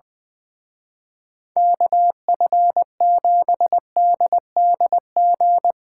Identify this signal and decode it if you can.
Morse code, heard as a keyed tone, about 23 wpm, 700 Hz. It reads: KF7DDG
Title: Speed Racing - USA 2x3 Call Signs 20wpm